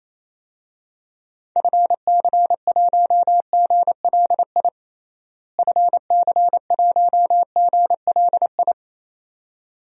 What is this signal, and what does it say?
Morse code, heard as a keyed tone, about 28 wpm, 700 Hz. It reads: FC1GLI FC1GLI
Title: DXCC-100 Call Signs - 6 Characters - Repeated 2x - 28wpm